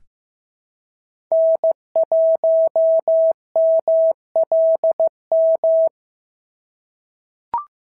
Morse code, heard as a keyed tone, about 15 wpm, 650 Hz. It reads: N1MLM
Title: SOTA QSO Practice 15wpm